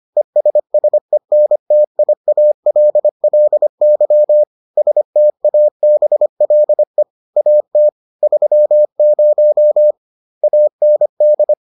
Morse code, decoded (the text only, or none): ESSENTIALLY STABLE AT 30 AND